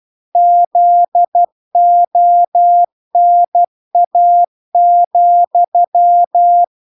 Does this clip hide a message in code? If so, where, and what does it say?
Yes: Morse code from 0.4 to 6.6 s, heard as a keyed tone, about 12 wpm, 700 Hz. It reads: ZONA,